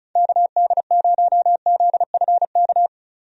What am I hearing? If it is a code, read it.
Morse code, heard as a keyed tone, about 34 wpm, 700 Hz. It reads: KD0ZFK
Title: USA Call Signs - 6 Characters - Repeated 3x - 35wpm